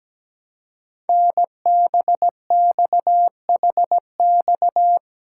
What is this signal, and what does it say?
Morse code, heard as a keyed tone, about 17 wpm, 700 Hz. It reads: NBXHX
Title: Code Groups - 5 Letters - 17wpm